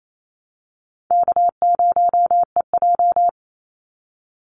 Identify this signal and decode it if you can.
Morse code, heard as a keyed tone, about 28 wpm, 700 Hz. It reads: K0EJ
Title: Easy Call Signs - 28wpm